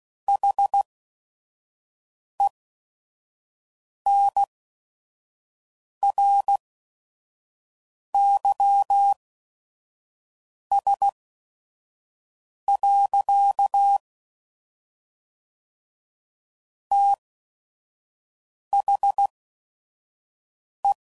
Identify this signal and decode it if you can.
Morse code, heard as a keyed tone, about 16 wpm, spaced as for 5 wpm, 800 Hz. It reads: HENRYS. THE
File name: CW Code Groups @5WPM / Technical Subjects / XL